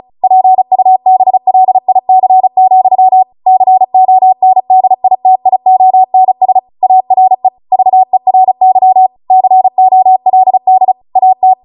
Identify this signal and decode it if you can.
Morse code, heard as a keyed tone, about 36 wpm, 750 Hz. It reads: PUBLIC, CONDITIONS ARE VERY COLD AT
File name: Morse / 35WPM